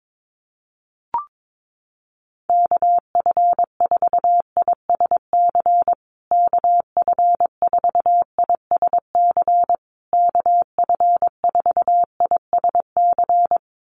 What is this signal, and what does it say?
Morse code, heard as a keyed tone, about 22 wpm, 700 Hz. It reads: KF4ISC KF4ISC KF4ISC
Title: USA Call Signs Repeated 3x 22wpm